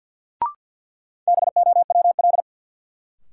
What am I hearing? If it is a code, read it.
Morse code, heard as a keyed tone, about 53 wpm, 700 Hz. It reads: BOWL
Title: Sentences from Top 1200 Words - New Word Drill - 50wpm